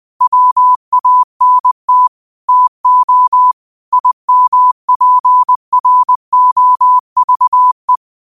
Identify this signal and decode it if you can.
Morse code, heard as a keyed tone, about 20 wpm, 1000 Hz. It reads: WANT TO IMPROVE
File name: qotdfiles / 20wpm